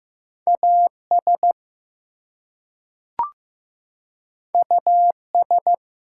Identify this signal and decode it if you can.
Morse code, heard as a keyed tone, about 15 wpm, 700 Hz. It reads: AS US
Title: Two Letter Words - 15wpm